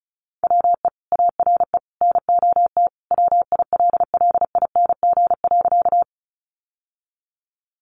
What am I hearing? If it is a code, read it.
Morse code, heard as a keyed tone, about 34 wpm, 700 Hz. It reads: WE ARE NOT WILLING.
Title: Sentences from Top 1000 Words 35wpm